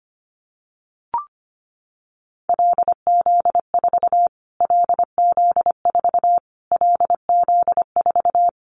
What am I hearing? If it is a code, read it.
Morse code, heard as a keyed tone, about 25 wpm, 700 Hz. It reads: LZ4 LZ4 LZ4
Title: Callsign Prefixes for Top 25 DXCC - Repeated 3x - 25wpm